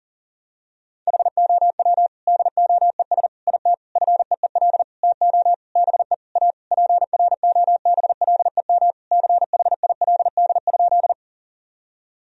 Morse code, heard as a keyed tone, about 40 wpm, 700 Hz. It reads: HOW DOES IT FEEL TO BE A PROBLEM CHILD?
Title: Sentences from Top 1000 Words - Review Part 2 - 40wpm